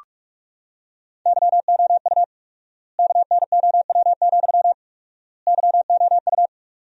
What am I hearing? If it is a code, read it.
Morse code, heard as a keyed tone, about 46 wpm, 700 Hz. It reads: YOU KNOW, YOU
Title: 4x Word-Spacing - Sentences from Top 200 Words - 45wpm